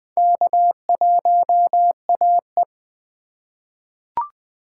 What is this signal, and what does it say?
Morse code, heard as a keyed tone, about 20 wpm, 700 Hz. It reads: K1AE
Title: K1USN SST QSO Practice - 20wpm